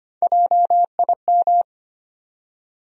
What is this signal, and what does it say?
Morse code, heard as a keyed tone, about 25 wpm, 700 Hz. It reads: JIM
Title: USA Common Names - Max 4 Letters - 25wpm